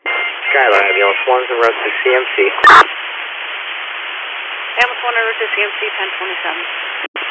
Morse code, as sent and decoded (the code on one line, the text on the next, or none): none